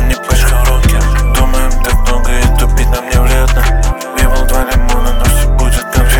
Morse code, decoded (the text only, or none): none